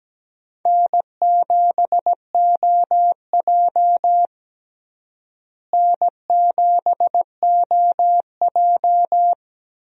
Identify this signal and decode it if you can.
Morse code, heard as a keyed tone, about 17 wpm, 700 Hz. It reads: N7OJ N7OJ
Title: USA Call Signs - 4 Characters - Repeated 2x - 17wpm